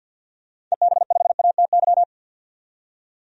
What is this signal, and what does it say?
Morse code, heard as a keyed tone, about 49 wpm, 700 Hz. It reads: EB5ATY
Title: DXCC-100 Call Signs - 6 Characters - 50wpm